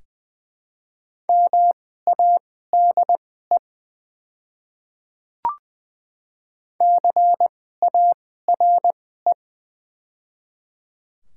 Morse code, heard as a keyed tone, about 20 wpm, spaced as for 14 wpm, 690 Hz. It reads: MADE CARE
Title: 2x Character Spacing - Top 300 Words - 20wpm